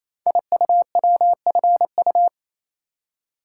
Morse code, heard as a keyed tone, about 29 wpm, 700 Hz. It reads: IUWFU
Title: Code Groups - 5 Letters - 28wpm